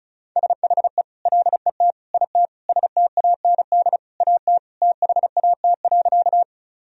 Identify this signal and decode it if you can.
Morse code, heard as a keyed tone, about 36 wpm, 690 Hz. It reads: SHE LET IT STAND AT THAT.
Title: Sentences from Top 1000 Words - Review Part 1 - 35wpm